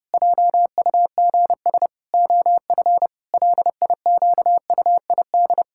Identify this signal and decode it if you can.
Morse code, heard as a keyed tone, about 30 wpm, 700 Hz. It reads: JUGS OF LIQUID
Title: Pangram Challenge v2 - 30wpm